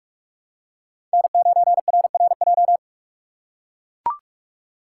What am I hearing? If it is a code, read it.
Morse code, heard as a keyed tone, about 42 wpm, 700 Hz. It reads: N9RRJ
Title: USA Call Signs - 5 Characters - 6x Character Spacing - 45wpm